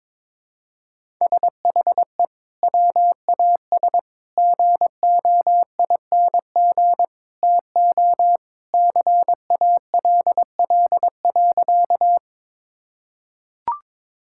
Morse code, heard as a keyed tone, about 21 wpm, 700 Hz. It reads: SHE WAS GOING TO CALL.
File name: practice / 22wpm / Sentences with 3-Word Phrases - 22wpm